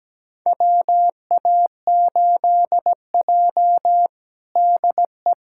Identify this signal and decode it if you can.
Morse code, heard as a keyed tone, about 17 wpm, 700 Hz. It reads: WA8J DE
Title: USA Callsign DE Callsign - 4 Characters - 17wpm